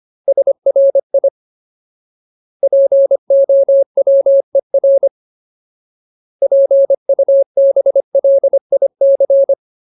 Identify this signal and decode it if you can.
Morse code, heard as a keyed tone, about 25 wpm, 550 Hz. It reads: SRI POWER PUBLIC
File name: Ultimate Word List / 25WPM Ultimate